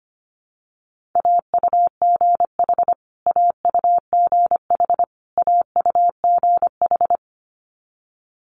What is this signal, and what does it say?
Morse code, heard as a keyed tone, about 25 wpm, 700 Hz. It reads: AUGH AUGH AUGH
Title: Multi-letter Phonemes Repeated 3x 25wpm